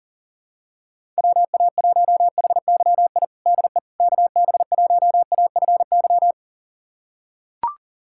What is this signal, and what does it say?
Morse code, heard as a keyed tone, about 40 wpm, 700 Hz. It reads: WA1HYI DE KB1AFY
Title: USA Callsign DE Callsign - 6 Characters - 40wpm